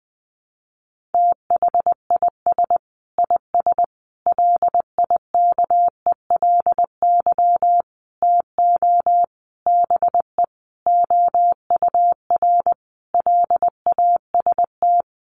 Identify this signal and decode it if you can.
Morse code, heard as a keyed tone, about 20 wpm, 700 Hz. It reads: THIS IS LIKELY TO BE OUR LAST
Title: Sentences from Top 1400 Words 20wpm